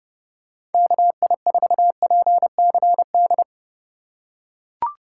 Morse code, heard as a keyed tone, about 29 wpm, 700 Hz. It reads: KI4PCD